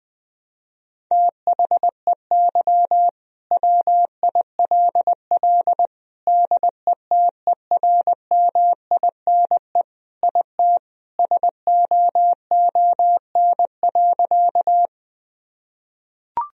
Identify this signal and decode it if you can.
Morse code, heard as a keyed tone, about 20 wpm, 700 Hz. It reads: THEY WILL DETERMINE IT SOON.
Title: Sentences from Top 1000 Words - Review Part 4 - 20wpm